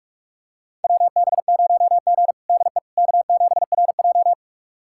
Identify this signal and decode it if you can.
Morse code, heard as a keyed tone, about 46 wpm, 700 Hz. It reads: WB0G DE K7RJ
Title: USA Callsign DE Callsign - 4 Characters - 45wpm